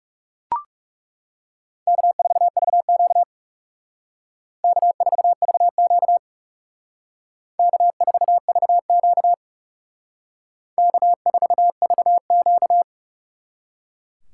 Morse code, heard as a keyed tone, about 36 wpm, 700 Hz. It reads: K4VQ K4VQ K4VQ K4VQ
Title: Speed Racing - USA 1x2 Call Signs 30wpm